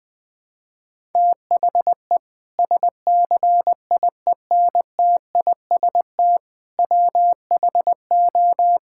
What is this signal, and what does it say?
Morse code, heard as a keyed tone, about 19 wpm, 700 Hz. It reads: THE SCIENTIST WHO
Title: Sentences with 8-Word Phrases - 20wpm